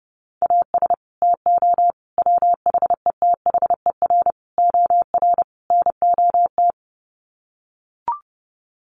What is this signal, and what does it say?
Morse code, heard as a keyed tone, about 30 wpm, 700 Hz. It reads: AS TO WHETHER OR NOT
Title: Sets of 5 Words Encore 30wpm